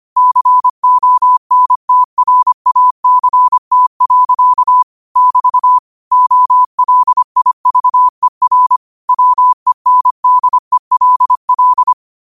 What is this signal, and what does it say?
Morse code, heard as a keyed tone, about 25 wpm, 1000 Hz. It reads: CONTRACT. <BT> OLIVER WENDELL
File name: 25wpm